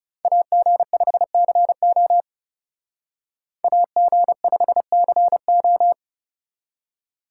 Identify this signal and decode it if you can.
Morse code, heard as a keyed tone, about 38 wpm, 700 Hz. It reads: AG5CO AG5CO